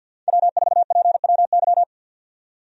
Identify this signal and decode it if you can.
Morse code, heard as a keyed tone, about 51 wpm, 700 Hz. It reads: W4PWY
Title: USA Call Signs - 5 Characters - 50wpm